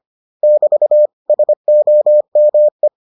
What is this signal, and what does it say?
Morse code, heard as a keyed tone, about 25 wpm, 600 Hz. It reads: <BT> SOME